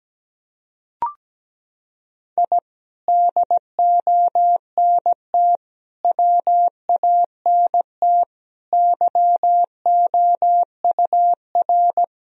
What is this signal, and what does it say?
Morse code, heard as a keyed tone, about 17 wpm, 700 Hz. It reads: I DONT WANT YOUR
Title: Sentences from Top 1000 Words - Review Part 3 - 17wpm